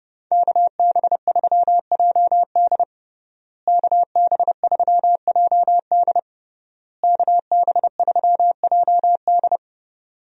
Morse code, heard as a keyed tone, about 30 wpm, 700 Hz. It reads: KB3JD KB3JD KB3JD
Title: USA Call Signs - 5 Characters - Repeated 3x - 30wpm